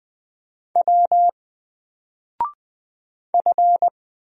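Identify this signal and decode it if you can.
Morse code, heard as a keyed tone, about 20 wpm, 700 Hz. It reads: W F
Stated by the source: Single letter and number Morse drill at 20 wpm, from N4LCD.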